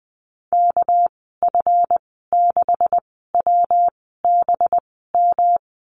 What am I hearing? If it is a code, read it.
Morse code, heard as a keyed tone, about 20 wpm, spaced as for 14 wpm, 700 Hz. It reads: KF6WBM
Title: 213 Lesson 34g - Number 8 - Call Signs 20wpm